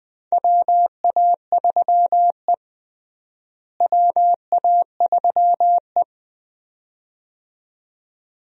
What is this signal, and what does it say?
Morse code, heard as a keyed tone, about 21 wpm, 700 Hz. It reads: WA3E WA3E
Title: USA Call Signs - 4 Characters - Repeated 2x - 20wpm